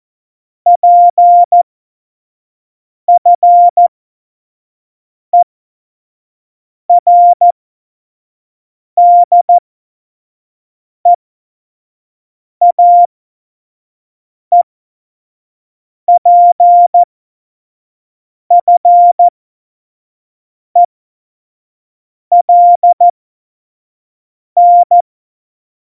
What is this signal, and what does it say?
Morse code, heard as a keyed tone, about 14 wpm, spaced as for 5 wpm, 700 Hz. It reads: PFERDEAEPFELN
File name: Morsekurs / Witze / Witz 14. Januar 2020, 25 F70